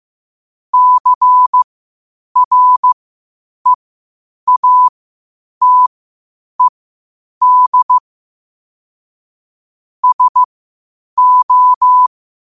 Morse code, heard as a keyed tone, about 15 wpm, spaced as for 8 wpm, 1000 Hz. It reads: CREATED SO
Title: 8 WPM morse code quotes for Fri, 01 Mar 2024